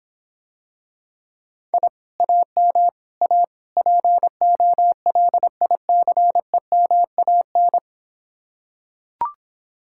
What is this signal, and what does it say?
Morse code, heard as a keyed tone, about 26 wpm, 700 Hz. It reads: I AM A POLICEMAN